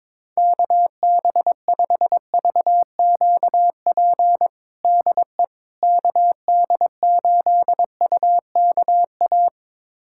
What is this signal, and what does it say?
Morse code, heard as a keyed tone, about 22 wpm, 700 Hz. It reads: KB5VQP DE KD8UKA
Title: USA Callsign DE Callsign - 6 Characters - 22wpm